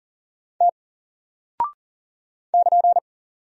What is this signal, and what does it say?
Morse code, heard as a keyed tone, about 38 wpm, 700 Hz. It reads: T <KN>
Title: Single Character (Letter:Number:Punctuation:Prosign) Alt 40wpm